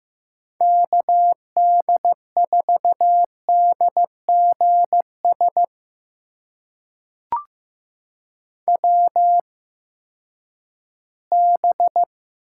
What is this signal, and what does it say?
Morse code, heard as a keyed tone, about 15 wpm, spaced as for 10 wpm, 700 Hz. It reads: KD4DGS WB
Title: USA Call Signs - 6 Characters - 8x Character Spacing - 15wpm